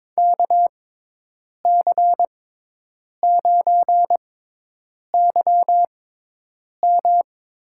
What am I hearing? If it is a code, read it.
Morse code, heard as a keyed tone, about 23 wpm, spaced as for 8 wpm, 700 Hz. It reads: KC9YM
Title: USA Call Signs - 6x Character Spacing - 22wpm